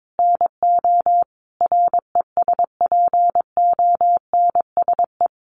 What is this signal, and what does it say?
Morse code, heard as a keyed tone, about 22 wpm, 700 Hz. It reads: NO RESPONSE
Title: Sentences from Top 1400 Words 22wpm